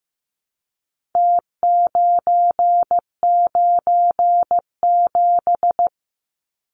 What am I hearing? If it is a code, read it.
Morse code, heard as a keyed tone, about 15 wpm, 700 Hz. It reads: T997
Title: Numbers - Contest Style - 4-Digit 15wpm